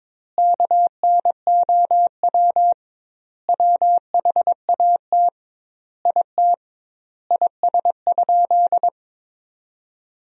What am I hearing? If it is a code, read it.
Morse code, heard as a keyed tone, about 22 wpm, 700 Hz. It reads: KNOW WHAT IT IS?
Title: 2x Word-Spacing - Sentences from Top 100 Words - 22wpm